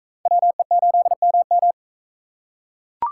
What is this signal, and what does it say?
Morse code, heard as a keyed tone, about 43 wpm, 700 Hz. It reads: WE8MM